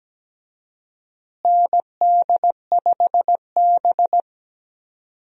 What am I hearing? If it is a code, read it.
Morse code, heard as a keyed tone, about 17 wpm, 700 Hz. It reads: ND5B
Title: USA Call Signs - 4 Characters - Repeated 3x - 17wpm